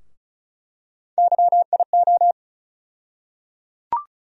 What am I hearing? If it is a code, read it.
Morse code, heard as a keyed tone, about 34 wpm, 700 Hz. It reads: YIO
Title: USA Call Sign Postfixes - 3 Letters 35wpm